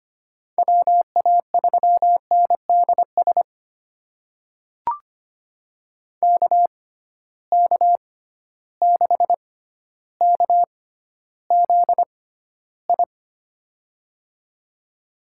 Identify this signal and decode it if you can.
Morse code, heard as a keyed tone, about 25 wpm, spaced as for 13 wpm, 700 Hz. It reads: WA3NDS KK6KZI
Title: USA Call Signs - 6x Character Spacing - 25wpm